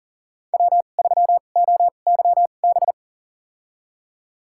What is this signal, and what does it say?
Morse code, heard as a keyed tone, about 40 wpm, spaced as for 29 wpm, 700 Hz. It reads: W3OYB
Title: USA Call Signs - 5 Characters - 2x Character Spacing - 40wpm